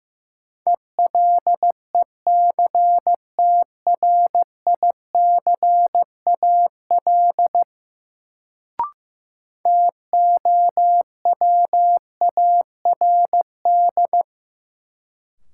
Morse code, heard as a keyed tone, about 15 wpm, 700 Hz. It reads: ELECTRICAL TOWARD
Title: Sentences from Top 1000 Words - Review Part 6 - New Word Drill - 15wpm